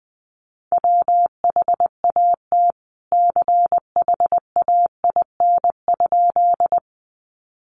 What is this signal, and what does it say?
Morse code, heard as a keyed tone, about 20 wpm, 700 Hz. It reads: WHAT CHAIN?
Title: Sentences from Top 1500 Words 20wpm